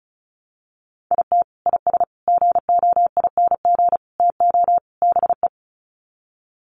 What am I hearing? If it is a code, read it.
Morse code, heard as a keyed tone, about 34 wpm, 700 Hz. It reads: IT IS GOING TO BE
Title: Sets of 5 Words 35wpm